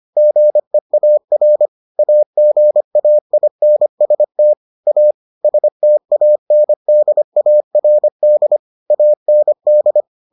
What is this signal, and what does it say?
Morse code, heard as a keyed tone, about 25 wpm, 600 Hz. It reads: GEAR AGAINST A STANDARD AND